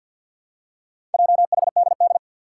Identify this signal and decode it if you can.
Morse code, heard as a keyed tone, about 47 wpm, 700 Hz. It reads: JHDD
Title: Four Challenging Letter-Number Combinations 50wpm